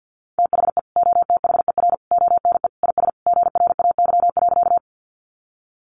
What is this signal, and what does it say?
Morse code, heard as a keyed tone, about 50 wpm, 700 Hz. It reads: THE OTHER ONE IS GRAY.
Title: Additional Easy Phrases 50wpm